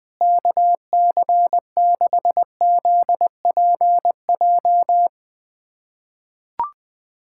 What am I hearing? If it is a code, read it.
Morse code, heard as a keyed tone, about 20 wpm, 700 Hz. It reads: KC6ZPJ
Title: USA Callsign DE Callsign - 6 Characters - 20wpm